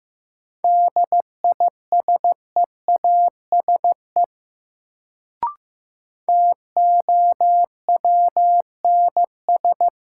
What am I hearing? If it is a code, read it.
Morse code, heard as a keyed tone, about 15 wpm, 700 Hz. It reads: DISEASE TOWNS
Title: Sentences from Top 1000 Words - Review Part 6 - New Word Drill - 15wpm